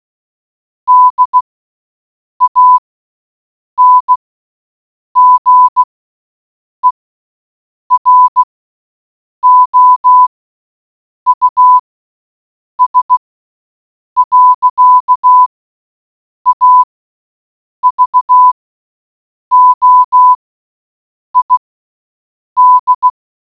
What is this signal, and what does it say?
Morse code, heard as a keyed tone, about 16 wpm, spaced as for 7 wpm, 1000 Hz. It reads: DANGEROUS. AVOID